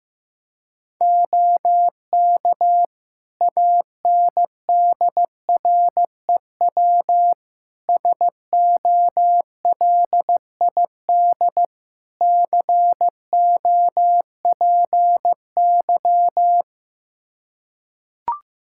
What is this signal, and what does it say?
Morse code, heard as a keyed tone, about 16 wpm, 700 Hz. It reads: OK ANDREW SOLID COPY
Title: Practice QSO - Exchange 3 - More Info - 15wpm